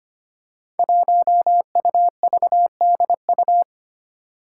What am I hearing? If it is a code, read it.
Morse code, heard as a keyed tone, about 25 wpm, 700 Hz. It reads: 1UVDU
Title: Five Challenging Letter-Number Combinations 25wpm